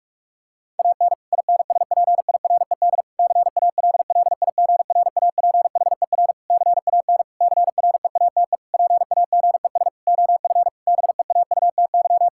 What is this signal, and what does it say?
Morse code, heard as a keyed tone, about 46 wpm, 700 Hz. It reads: AN INSPIRED CALLIGRAPHER CAN CREATE PAGES OF BEAUTY
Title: Pangram Challenge v2 - 45wpm